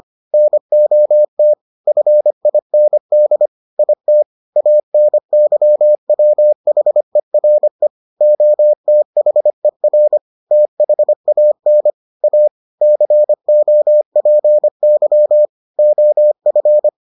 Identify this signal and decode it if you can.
Morse code, heard as a keyed tone, about 25 wpm, 600 Hz. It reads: NOT FIND IT ANYWHERE OTHER THAN A COPY OF